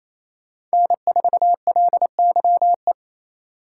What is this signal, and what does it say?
Morse code, heard as a keyed tone, about 35 wpm, 700 Hz. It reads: N4LYE